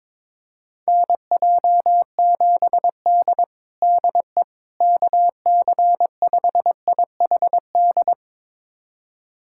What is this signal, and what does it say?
Morse code, heard as a keyed tone, about 22 wpm, 700 Hz. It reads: NJ7D DE KC5IHD
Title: USA Callsign DE Callsign 22wpm